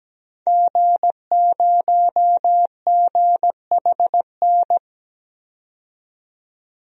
Morse code, heard as a keyed tone, about 17 wpm, 700 Hz. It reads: G0GHN
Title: DXCC-100 Call Signs - 5 Characters - 17wpm